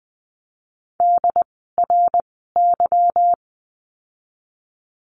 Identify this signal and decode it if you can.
Morse code, heard as a keyed tone, about 19 wpm, 700 Hz. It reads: D R Y
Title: 146 Lesson 24f - Letter y - Words 20wpm